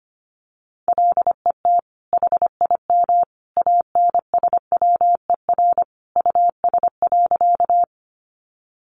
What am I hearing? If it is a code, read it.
Morse code, heard as a keyed tone, about 25 wpm, 700 Hz. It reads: LET HIM ANSWER US.
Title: Sentences from Top 300 Words 25wpm